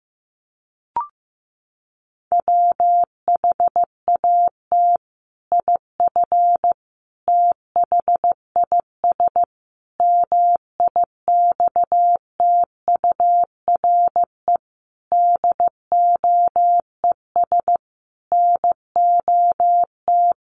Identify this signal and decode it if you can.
Morse code, heard as a keyed tone, about 15 wpm, 700 Hz. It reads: WHAT IF THIS MIXTURE DOES NOT
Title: Sentences from Top 1500 Words 15wpm